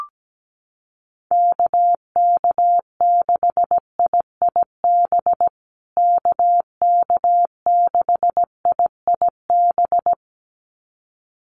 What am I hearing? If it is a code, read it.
Morse code, heard as a keyed tone, about 17 wpm, 700 Hz. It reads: KK6IIB KK6IIB
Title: USA Call Signs Repeated 2x 17wpm